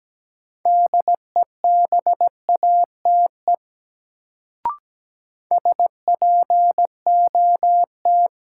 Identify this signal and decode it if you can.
Morse code, heard as a keyed tone, about 17 wpm, 700 Hz. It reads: DEBATE SPOT
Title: Sentences from Top 1100 Words - New Word Drill Alt - 17wpm